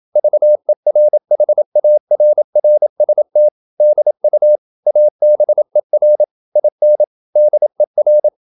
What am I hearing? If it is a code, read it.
Morse code, heard as a keyed tone, about 27 wpm, 600 Hz. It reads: VERHARRST DU ABER IN DER